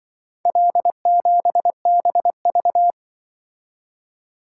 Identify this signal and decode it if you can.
Morse code, heard as a keyed tone, about 24 wpm, 700 Hz. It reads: L7BV
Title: Speed-Racing - Four Letter-Number Combinations - 20wpm